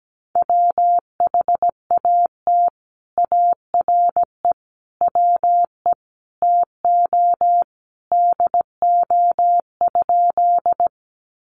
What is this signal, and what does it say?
Morse code, heard as a keyed tone, about 17 wpm, 700 Hz. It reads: WHAT ARE WE TO DO?
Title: Sentences from Top 100 Words 17wpm